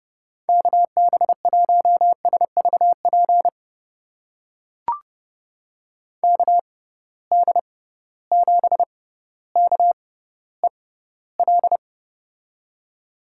Morse code, heard as a keyed tone, about 30 wpm, spaced as for 15 wpm, 700 Hz. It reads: KB1SVP KD7KEL
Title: USA Call Signs - 6x Character Spacing - 30wpm